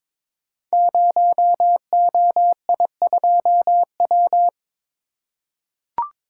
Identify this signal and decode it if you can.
Morse code, heard as a keyed tone, about 22 wpm, 700 Hz. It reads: 0OI2W
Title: Code Groups - 5 Letter:Numbers - 22wpm